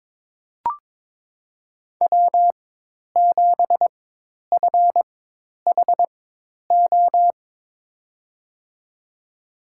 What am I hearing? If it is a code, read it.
Morse code, heard as a keyed tone, about 22 wpm, spaced as for 10 wpm, 700 Hz. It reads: W7FHO
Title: USA Call Signs - 4x Character Spacing - 22wpm